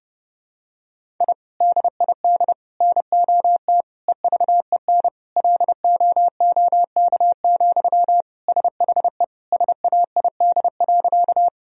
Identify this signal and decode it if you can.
Morse code, heard as a keyed tone, about 30 wpm, 700 Hz. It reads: I DID NOT EVEN LOOK, SHE SAID.
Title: Sentences from Top 1000 Words - Review Part 1 - 30wpm